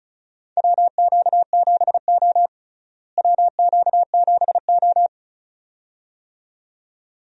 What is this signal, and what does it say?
Morse code, heard as a keyed tone, about 36 wpm, 700 Hz. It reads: WQ7O WQ7O
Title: USA Call Signs - 4 Characters - Repeated 2x - 35wpm